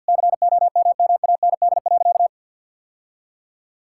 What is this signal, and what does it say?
Morse code, heard as a keyed tone, about 52 wpm, 700 Hz. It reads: COMMAND.
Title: Sentences from Top 1600 Words 50wpm